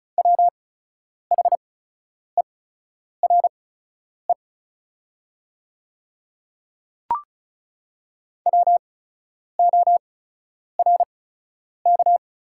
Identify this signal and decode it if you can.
Morse code, heard as a keyed tone, about 36 wpm, spaced as for 10 wpm, 700 Hz. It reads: WHERE WORK
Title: 8x Character Spacing - Top 200 Words - 35wpm